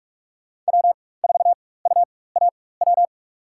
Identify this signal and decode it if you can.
Morse code, heard as a keyed tone, about 44 wpm, spaced as for 21 wpm, 700 Hz. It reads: W4UAW
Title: USA Call Signs - 4x Character Spacing - 45wpm